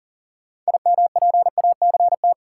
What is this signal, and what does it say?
Morse code, heard as a keyed tone, about 40 wpm, 700 Hz. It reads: IMPACT
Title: Sentences from Top 1000 Words - Review Part 6 - New Word Drill - 40wpm